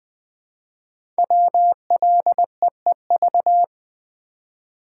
Morse code, heard as a keyed tone, about 20 wpm, 700 Hz. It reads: WLEEV
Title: Code Groups - 5 Letters - 20wpm